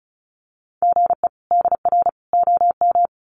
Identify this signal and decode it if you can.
Morse code, heard as a keyed tone, about 34 wpm, 700 Hz. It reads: GE DR OM
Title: Common QSO Phrases 35wpm